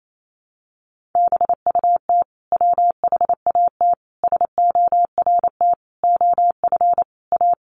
Morse code, heard as a keyed tone, about 28 wpm, 700 Hz. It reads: BUT WHAT SORT OF A
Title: Sentences from Top 1300 Words 28wpm